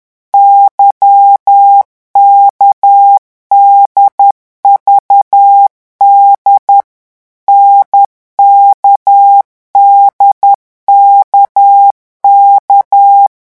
Morse code, heard as a keyed tone, about 10 wpm, 800 Hz. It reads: YKDVD NKDKK
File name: morse mp3 / revision3-vitesse 10 mots